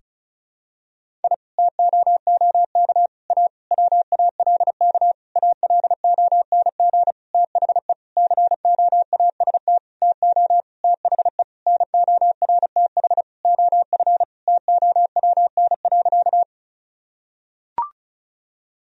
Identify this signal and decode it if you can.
Morse code, heard as a keyed tone, about 36 wpm, 700 Hz. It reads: I TOOK A WALK ALONG THE COAST TO THE NORTH OF TOWN.
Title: Sentences from Top 1000 Words - Review Part 4 - 35wpm